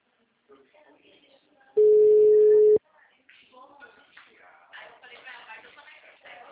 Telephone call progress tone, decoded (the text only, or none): ringback tone